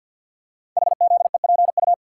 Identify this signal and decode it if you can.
Morse code, heard as a keyed tone, about 49 wpm, 700 Hz. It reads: HZEPU
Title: Code Groups - 5 Letters - 50wpm